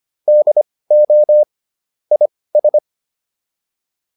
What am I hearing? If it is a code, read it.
Morse code, heard as a keyed tone, about 25 wpm, spaced as for 18 wpm, 600 Hz. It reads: DO IS
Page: intermediate-practice-files